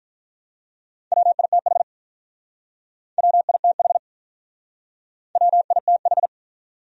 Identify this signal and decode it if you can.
Morse code, heard as a keyed tone, about 48 wpm, 700 Hz. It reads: WITH WITH WITH